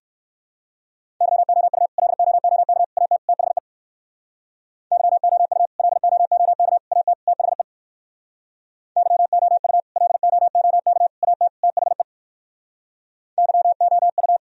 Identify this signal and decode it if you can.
Morse code, heard as a keyed tone, about 56 wpm, 700 Hz. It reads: YOU LOOK AT THE YOU LOOK AT THE YOU LOOK AT THE YOU
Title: Speed-Racing - Sets of 4 Words 45wpm